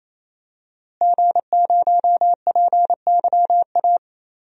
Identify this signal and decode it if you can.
Morse code, heard as a keyed tone, about 27 wpm, 700 Hz. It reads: G0PYA
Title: DXCC-100 Call Signs - 5 Characters - Repeated 3x - 28wpm